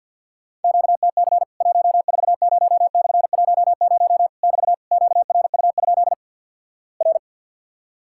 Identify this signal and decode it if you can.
Morse code, heard as a keyed tone, about 53 wpm, 700 Hz. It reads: QTC 140/10 <BT> QRU? R
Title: Worked All Europe DX Contest - QTC Exchange with Repeats - 50wpm